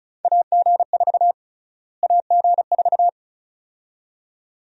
Morse code, heard as a keyed tone, about 36 wpm, 700 Hz. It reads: AG4 AG4
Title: USA Call Sign Prefixes - Repeated 2x - 35wpm